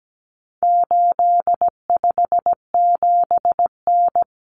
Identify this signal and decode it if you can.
Morse code, heard as a keyed tone, about 17 wpm, 700 Hz. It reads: 857N
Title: Common Cut Numbers - 4-Digit 17wpm